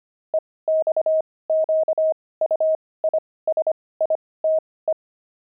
Morse code, heard as a keyed tone, about 25 wpm, spaced as for 18 wpm, 630 Hz. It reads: EXQUISITE
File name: pangram-fs-18-cs-25-opal